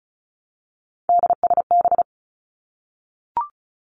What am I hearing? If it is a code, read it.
Morse code, heard as a keyed tone, about 36 wpm, 700 Hz. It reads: DSB
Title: Three Challenging Letter-Number Combinations 35wpm